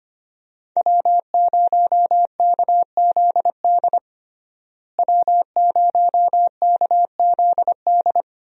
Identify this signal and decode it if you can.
Morse code, heard as a keyed tone, about 25 wpm, 690 Hz. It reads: W0KZD W0KZD
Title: USA Call Signs - 5 Characters - Repeated 2x - 25wpm